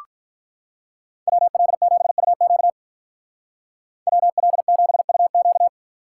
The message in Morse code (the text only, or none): WL7UQ WL7UQ